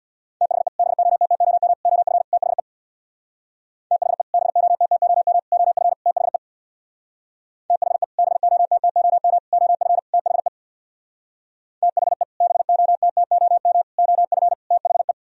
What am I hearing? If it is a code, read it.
Morse code, heard as a keyed tone, about 64 wpm, 700 Hz. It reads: THE BOTTOM OF THE THE BOTTOM OF THE THE BOTTOM OF THE THE BOTTOM OF THE